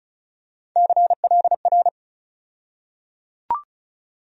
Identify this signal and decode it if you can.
Morse code, heard as a keyed tone, about 34 wpm, 700 Hz. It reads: CLR
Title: USA Call Sign Postfixes - 3 Letters 35wpm